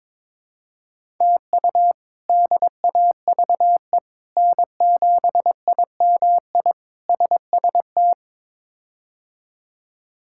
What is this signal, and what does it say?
Morse code, heard as a keyed tone, about 21 wpm, 700 Hz. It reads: TU DAVE N7IMI SST
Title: K1USN SST QSO Practice - 22wpm